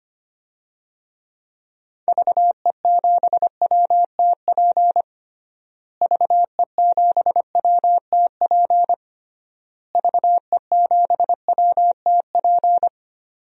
Morse code, heard as a keyed tone, about 25 wpm, 700 Hz. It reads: VE7WTP VE7WTP VE7WTP
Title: DXCC-100 Call Signs - 6 Characters - Repeated 3x - 25wpm